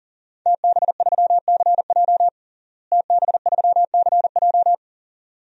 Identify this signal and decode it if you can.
Morse code, heard as a keyed tone, about 40 wpm, 700 Hz. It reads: TB3CJ TB3CJ
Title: DXCC-100 Call Signs - 5 Characters - Repeated 2x - 40wpm